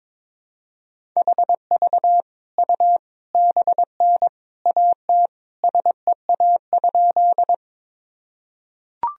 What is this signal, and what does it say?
Morse code, heard as a keyed tone, about 22 wpm, 700 Hz. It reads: HV U BN AT SEA?
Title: Sentences from Top 300 Words and Abbreviations 22wpm